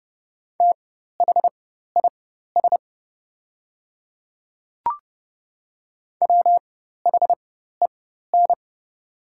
Morse code, heard as a keyed tone, about 30 wpm, spaced as for 14 wpm, 700 Hz. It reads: THIS WHEN
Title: 4x Character Spacing - Top 100 Words - 30wpm